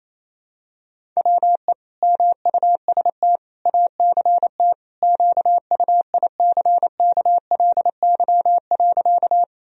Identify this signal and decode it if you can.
Morse code, heard as a keyed tone, about 28 wpm, 700 Hz. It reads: WE MUST ACT QUICKLY.
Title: Sentences from Top 1000 Words - Review Part 1 - 28wpm